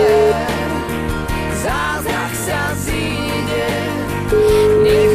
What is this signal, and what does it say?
Ringback tone (Europe, 425 Hz), with noise.